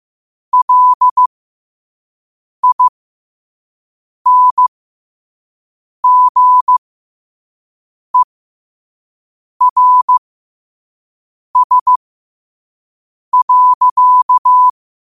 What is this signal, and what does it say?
Morse code, heard as a keyed tone, about 15 wpm, spaced as for 5 wpm, 1000 Hz. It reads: LINGERS.
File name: qotdfiles / 5wpm